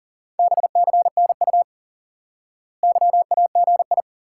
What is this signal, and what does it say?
Morse code, heard as a keyed tone, about 40 wpm, 700 Hz. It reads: BCNU YAGI
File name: IQE Course 3 - Lesson 89 - YAGI - Two Elements - 40wpm